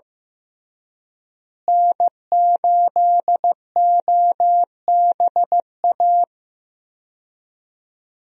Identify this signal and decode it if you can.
Morse code, heard as a keyed tone, about 16 wpm, 700 Hz. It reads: N8OBA
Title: USA Call Signs - 5 Characters - Repeated 3x - 15wpm